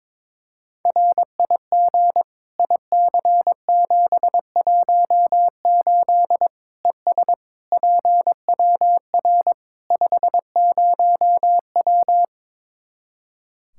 Morse code, heard as a keyed tone, about 22 wpm, 700 Hz. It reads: RIG IC718 ES PWR 50W
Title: Practice QSO - Exchange 2 - Info - 22wpm